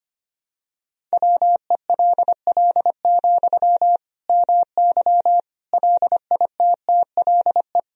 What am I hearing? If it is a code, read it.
Morse code, heard as a keyed tone, about 25 wpm, 700 Hz. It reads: WELL, MY LITTLE
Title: Sentences from Top 1000 Words - Review Part 2 - 25wpm